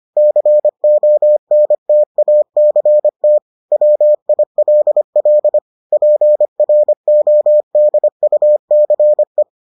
Morse code, heard as a keyed tone, about 25 wpm, 600 Hz. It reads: CONTACT WILL PRODUCE